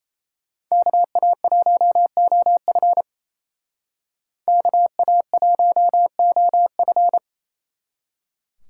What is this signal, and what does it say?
Morse code, heard as a keyed tone, about 32 wpm, 700 Hz. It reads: KA1OF KA1OF
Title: Speed Racing - USA 2x2 Call Signs 28wpm